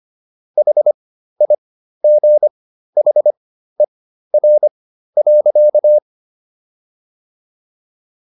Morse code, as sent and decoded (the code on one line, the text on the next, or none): .... .. --. .... .-. .-.-.-
HIGHER.